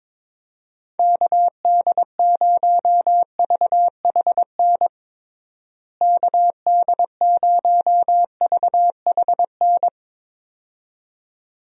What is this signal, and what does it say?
Morse code, heard as a keyed tone, about 22 wpm, 700 Hz. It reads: KD0VHN KD0VHN
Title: USA Call Signs - 6 Characters - Repeated 2x - 22wpm